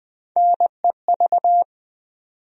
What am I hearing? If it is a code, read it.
Morse code, heard as a keyed tone, about 20 wpm, 700 Hz. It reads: NEV
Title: USA Call Sign Postfixes - 3 Letters 20wpm